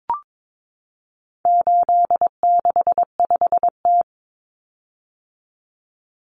Morse code, heard as a keyed tone, about 22 wpm, 700 Hz. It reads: 865T
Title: Common Cut Numbers - 4-Digit 22wpm